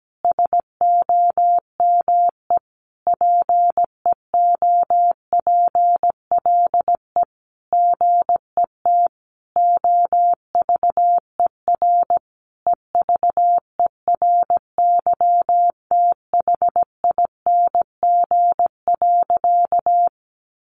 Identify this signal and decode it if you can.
Morse code, heard as a keyed tone, about 17 wpm, 700 Hz. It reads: SOME PEOPLE GET OVER EVERYTHING.
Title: Sentences from Top 900 Words 17wpm